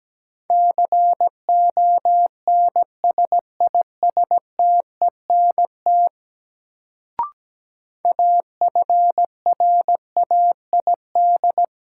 Morse code, heard as a keyed tone, about 17 wpm, 700 Hz. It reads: CONSISTENT AFRAID
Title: Sentences from Top 1500 Words - New Word Drill - 17wpm